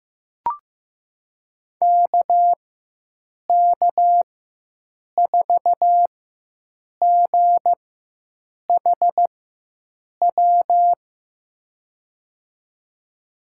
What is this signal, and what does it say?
Morse code, heard as a keyed tone, about 15 wpm, spaced as for 7 wpm, 700 Hz. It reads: KK4GHW
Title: USA Call Signs - 6 Characters - 4x Character Spacing - 15wpm